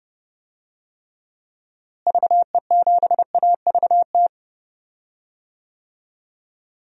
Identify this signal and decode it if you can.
Morse code, heard as a keyed tone, about 30 wpm, 700 Hz. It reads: VE7AVT
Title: DXCC-100 Call Signs - 6 Characters - Repeated 3x - 30wpm